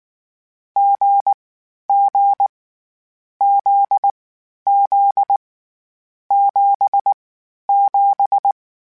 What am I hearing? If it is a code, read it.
Morse code, heard as a keyed tone, about 19 wpm, 800 Hz. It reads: G G Z Z 7 7
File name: Difficult Combos / Difficult Combos-mp3s / awj1 ndb6 gz7 @18